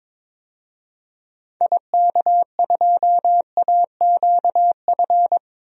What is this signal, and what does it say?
Morse code, heard as a keyed tone, about 22 wpm, 700 Hz. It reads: IK2AQF
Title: DXCC-100 Call Signs - 6 Characters - 22wpm